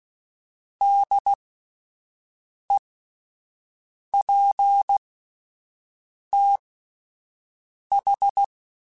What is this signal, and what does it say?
Morse code, heard as a keyed tone, about 16 wpm, spaced as for 5 wpm, 800 Hz. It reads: DEPTH